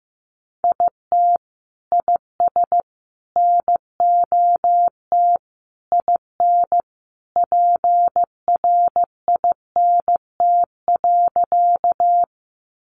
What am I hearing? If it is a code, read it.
Morse code, heard as a keyed tone, about 15 wpm, 700 Hz. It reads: IT IS NOT IN PRINT.
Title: Sentences from Top 1400 Words 15wpm